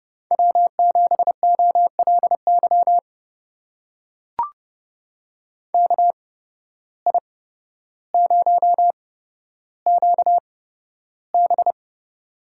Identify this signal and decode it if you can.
Morse code, heard as a keyed tone, about 31 wpm, spaced as for 13 wpm, 700 Hz. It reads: W7OLY KI0QB
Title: USA Call Signs - 8x Character Spacing - 30wpm